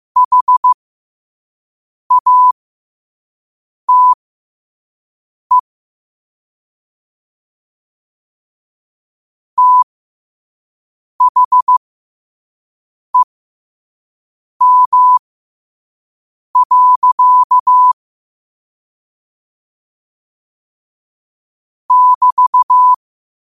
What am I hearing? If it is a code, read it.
Morse code, heard as a keyed tone, about 15 wpm, spaced as for 5 wpm, 1000 Hz. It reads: HATE THEM. <BT>
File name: qotdfiles / 5wpm